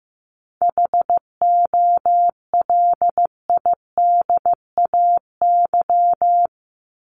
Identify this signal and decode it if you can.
Morse code, heard as a keyed tone, about 15 wpm, 700 Hz. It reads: HOLIDAY
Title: Top 1501-1750 Words 15wpm